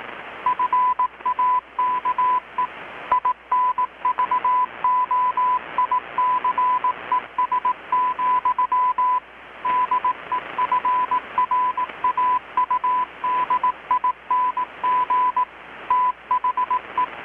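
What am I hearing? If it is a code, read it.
Morse code, heard as a keyed tone, about 18 wpm, 1000 Hz, with radio noise. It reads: FAKE INVOICES, DEFRAUDING THE
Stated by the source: Morse from KPH's Night of Nights XX maritime radiotelegraphy event, received in Chiapas, Mexico about 3,800 km away on a Kenwood R-600 with a grounded 10 m long-wire antenna.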